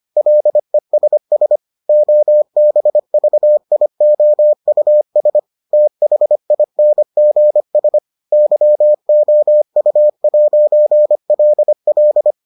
Morse code, heard as a keyed tone, about 25 wpm, 600 Hz. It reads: LESS OBVIOUS THINGS YOU'LL